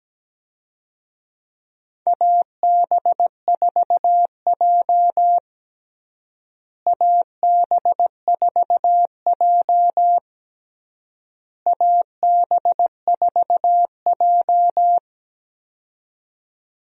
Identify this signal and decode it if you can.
Morse code, heard as a keyed tone, about 17 wpm, 700 Hz. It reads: AB4J AB4J AB4J
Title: USA Call Signs - 4 Characters - Repeated 3x - 17wpm